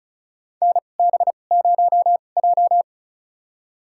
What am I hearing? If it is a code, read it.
Morse code, heard as a keyed tone, about 36 wpm, spaced as for 25 wpm, 700 Hz. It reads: NB0J
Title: USA Call Signs - 2x Character Spacing - 35wpm